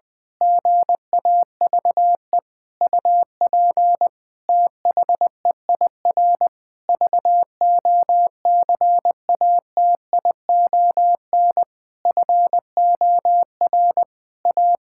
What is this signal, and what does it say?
Morse code, heard as a keyed tone, about 20 wpm, 700 Hz. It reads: GAVE UP THEIR VOCATION FOR A
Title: Pangram Challenge v2 - 20wpm